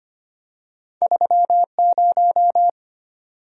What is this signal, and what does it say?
Morse code, heard as a keyed tone, about 25 wpm, 700 Hz. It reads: 30
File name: Two Letter-Number Combinations - 25wpm